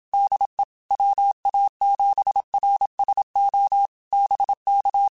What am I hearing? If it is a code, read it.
Morse code, heard as a keyed tone, about 27 wpm, 800 Hz. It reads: DE WA7RSO BK